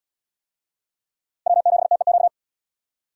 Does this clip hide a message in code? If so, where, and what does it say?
Yes: Morse code from 1.5 to 2.3 s, heard as a keyed tone, about 77 wpm, 700 Hz. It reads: W7TEQ